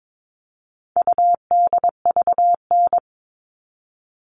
Morse code, heard as a keyed tone, about 22 wpm, 700 Hz. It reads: UDVN
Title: All Cut Numbers - 4-Digit 22wpm